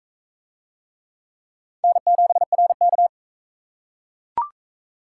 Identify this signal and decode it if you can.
Morse code, heard as a keyed tone, about 40 wpm, 700 Hz. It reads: N7RK